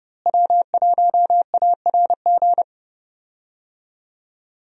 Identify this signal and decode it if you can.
Morse code, heard as a keyed tone, about 30 wpm, 700 Hz. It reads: W1ARG
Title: USA Call Signs - 5 Characters - Repeated 2x - 30wpm